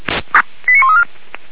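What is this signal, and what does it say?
Five-tone selective call (ZVEI), heard as quick runs of five tones, tones of about 80 ms. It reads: 89134
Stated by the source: ZVEI five-tone selective call received by SDR, from the Feuerwehr Orlamünde fire brigade audio archive.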